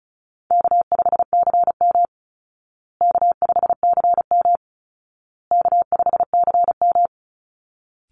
Morse code, heard as a keyed tone, about 34 wpm, 700 Hz. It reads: K5CM K5CM K5CM
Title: Easy Call Signs Repeated 3x - 35wpm